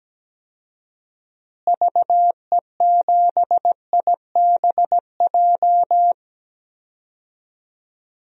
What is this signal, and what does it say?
Morse code, heard as a keyed tone, about 17 wpm, 700 Hz. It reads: VE7IBJ
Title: DXCC-100 Call Signs - 6 Characters - Repeated 3x - 17wpm